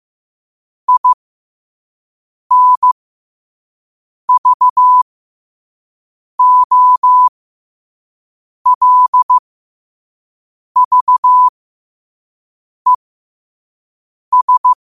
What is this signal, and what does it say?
Morse code, heard as a keyed tone, about 15 wpm, spaced as for 5 wpm, 1000 Hz. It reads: INVOLVES
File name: qotdfiles / 5wpm